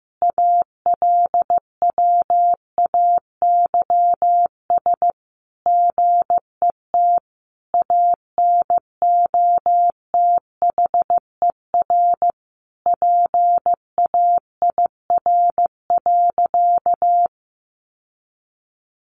Morse code, heard as a keyed tone, about 15 wpm, 700 Hz. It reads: ALWAYS GET ANOTHER PAIR.
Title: Sentences from Top 1200 Words 15wpm